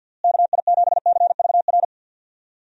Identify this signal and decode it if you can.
Morse code, heard as a keyed tone, about 48 wpm, 700 Hz. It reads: KI7CVR
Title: USA Call Signs - 6 Characters - Repeated 2x - 50wpm